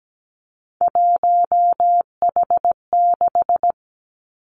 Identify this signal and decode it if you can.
Morse code, heard as a keyed tone, about 17 wpm, 700 Hz. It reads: 1H6
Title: Three Challenging Letter-Number Combinations 17wpm